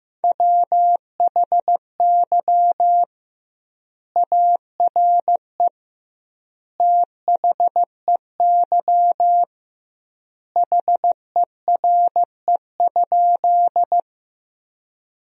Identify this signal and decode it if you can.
Morse code, heard as a keyed tone, about 16 wpm, 700 Hz. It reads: WHY ARE THEY HERE?
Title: 2x Word-Spacing - Sentences from Top 200 Words - 15wpm